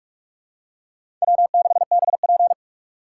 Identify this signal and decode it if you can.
Morse code, heard as a keyed tone, about 44 wpm, 690 Hz. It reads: W6BP
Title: USA Call Signs - 4 Characters - 2x Character Spacing - 45wpm